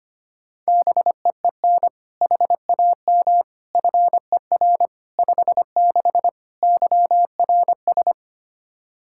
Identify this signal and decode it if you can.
Morse code, heard as a keyed tone, about 25 wpm, 700 Hz. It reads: BEEN HAM FER 56 YRS
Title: Practice QSO - Exchange 3 - More Info - 25wpm